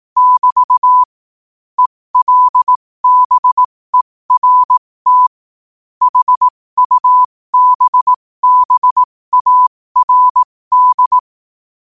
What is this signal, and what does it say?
Morse code, heard as a keyed tone, about 18 wpm, spaced as for 15 wpm, 1000 Hz. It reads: <BT> ELBERT HUBBARD
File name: qotdfiles / 15wpm